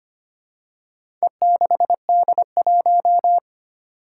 Morse code, heard as a keyed tone, about 25 wpm, 700 Hz. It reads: E6D1
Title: Four Challenging Letter-Number Combinations 25wpm